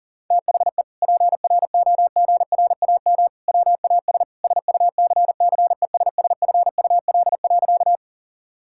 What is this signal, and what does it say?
Morse code, heard as a keyed tone, about 40 wpm, 700 Hz. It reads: THE PROGRAM WAS SUCCESSFUL.
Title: Sentences from Top 1000 Words - Review Part 5 - 40wpm